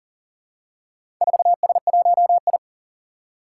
Morse code, heard as a keyed tone, about 40 wpm, 700 Hz. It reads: 4S1I
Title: Four Challenging Letter-Number Combinations 40wpm